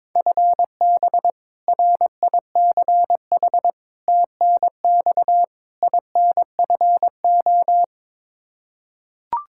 Morse code, heard as a keyed tone, about 22 wpm, 700 Hz. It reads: FB RICH TNX INFO
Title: Practice QSO - Exchange 3 - More Info - 22wpm